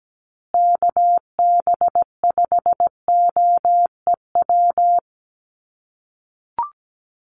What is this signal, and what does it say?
Morse code, heard as a keyed tone, about 17 wpm, 700 Hz. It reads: KB5OEW
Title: USA Call Signs Repeated 3x 17wpm